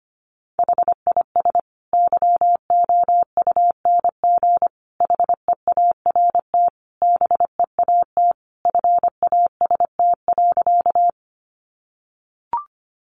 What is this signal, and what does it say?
Morse code, heard as a keyed tone, about 25 wpm, 700 Hz. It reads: HIS YOUNG HEART BEAT FAST.
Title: Sentences from Top 1300 Words 25wpm